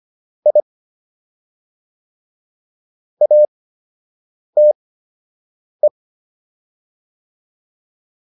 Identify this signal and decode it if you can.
Morse code, heard as a keyed tone, about 25 wpm, spaced as for 7 wpm, 600 Hz. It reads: I ATE